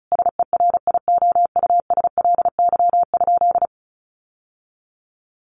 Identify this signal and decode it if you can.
Morse code, heard as a keyed tone, about 34 wpm, 700 Hz. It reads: SERIOUSLY?
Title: Sentences from Top 1200 Words 35wpm